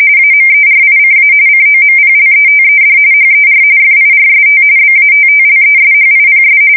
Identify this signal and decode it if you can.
RTTY, about 45.45 baud. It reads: AMERICA COMPLETED HER FIRST BATTLESHIPS
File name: fsk170 / battleshipnj